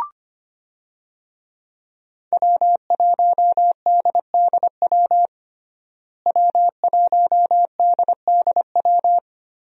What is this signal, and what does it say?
Morse code, heard as a keyed tone, about 25 wpm, 700 Hz. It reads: W1DDW W1DDW
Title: USA Call Signs - 5 Characters - Repeated 2x - 25wpm